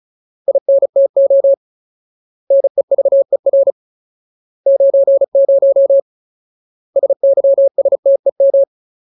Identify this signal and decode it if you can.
Morse code, heard as a keyed tone, about 34 wpm, 550 Hz. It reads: INTO NEVER 90 SYSTEM
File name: Ultimate Word List / 35WPM Ultimate